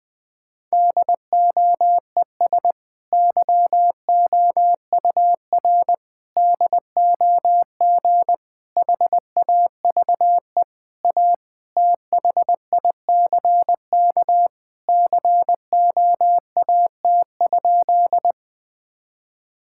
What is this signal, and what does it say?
Morse code, heard as a keyed tone, about 21 wpm, 700 Hz. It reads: DOES YOUR DOG HAVE A THICK COAT?
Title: Sentences from Top 1600 Words 20wpm